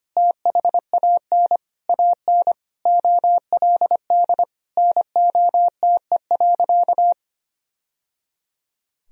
Morse code, heard as a keyed tone, about 25 wpm, 700 Hz. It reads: THAN AN OLD NOTE.
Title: Sentences from Top 1000 Words - Review Part 2 - 25wpm